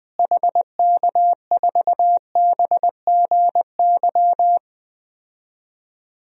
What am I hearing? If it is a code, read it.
Morse code, heard as a keyed tone, about 21 wpm, 700 Hz. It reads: HK4BGY
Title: DXCC-100 Call Signs - 6 Characters - Repeated 3x - 20wpm